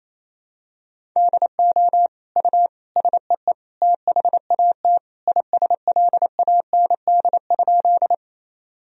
Morse code, heard as a keyed tone, about 28 wpm, 700 Hz. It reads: DO U SEE THAT ISLAND?
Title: Sentences from Top 500 Words and Abbreviations 28wpm